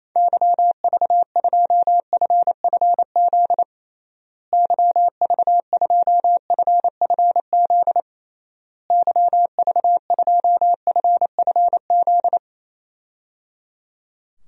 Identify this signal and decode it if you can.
Morse code, heard as a keyed tone, about 28 wpm, 700 Hz. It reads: YV2FFZ YV2FFZ YV2FFZ
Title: DXCC-100 Call Signs - 6 Characters - Repeated 3x - 28wpm